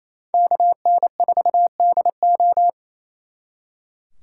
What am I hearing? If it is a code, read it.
Morse code, heard as a keyed tone, about 35 wpm, 700 Hz. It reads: KN4DO
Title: Speed Racing - USA 2x2 Call Signs 28wpm